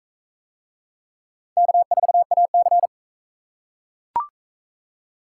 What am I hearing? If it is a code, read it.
Morse code, heard as a keyed tone, about 41 wpm, 700 Hz. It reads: K4AC